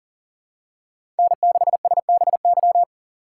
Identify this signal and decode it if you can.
Morse code, heard as a keyed tone, about 40 wpm, 700 Hz. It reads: N6SBY
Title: USA Call Signs - 5 Characters - 6x Character Spacing - 40wpm